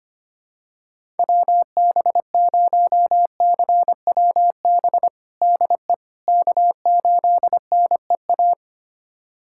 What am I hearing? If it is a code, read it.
Morse code, heard as a keyed tone, about 25 wpm, 700 Hz. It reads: WB0CWB DE K8NEA
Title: USA Callsign DE Callsign 25wpm